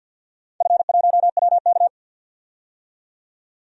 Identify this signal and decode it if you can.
Morse code, heard as a keyed tone, about 53 wpm, 700 Hz. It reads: F1WK
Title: DXCC-100 Call Signs - 4 Characters - 50wpm